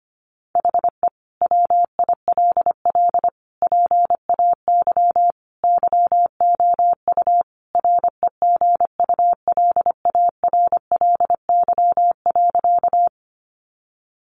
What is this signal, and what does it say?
Morse code, heard as a keyed tone, about 25 wpm, 700 Hz. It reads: HE WILL PAY YOU REGULARLY.
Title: Sentences from Top 1400 Words 25wpm